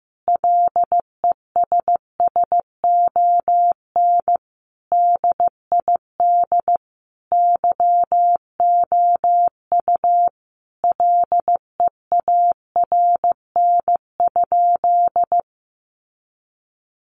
Morse code, heard as a keyed tone, about 15 wpm, 700 Hz. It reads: LESSON DID YOU LEARN?
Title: Sentences from Top 1200 Words 15wpm